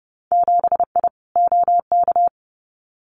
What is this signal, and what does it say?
Morse code, heard as a keyed tone, about 30 wpm, 700 Hz. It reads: 7I OK
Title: Winter Field Day Exchanges Encore 30wpm